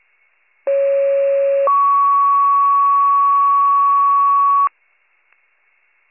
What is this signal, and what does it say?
Two-tone page, heard as a one-second tone, then a three-second tone, about 550 then 1120 Hz. It is heard from 0.7 to 4.7 s.